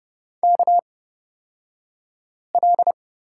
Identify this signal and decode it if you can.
Morse code, heard as a keyed tone, about 30 wpm, spaced as for 5 wpm, 700 Hz. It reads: KL
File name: Lesson 37d - Letter x - Two Characters 30wpm